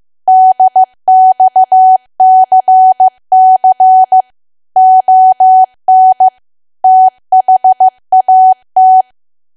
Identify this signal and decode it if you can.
Morse code, heard as a keyed tone, about 15 wpm, 750 Hz. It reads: DXCC ON THAT
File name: cwa / W1AW-Level2 / 15WPM